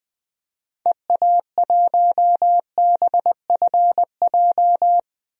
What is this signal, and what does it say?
Morse code, heard as a keyed tone, about 20 wpm, 700 Hz. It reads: EA1BFJ
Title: DXCC-100 Call Signs - 6 Characters - 20wpm